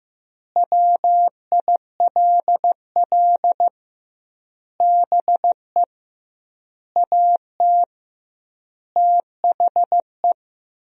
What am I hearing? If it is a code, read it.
Morse code, heard as a keyed tone, about 15 wpm, 700 Hz. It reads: WILL BE AT THE
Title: 2x Word-Spacing - Sentences from Top 400 Words - 15wpm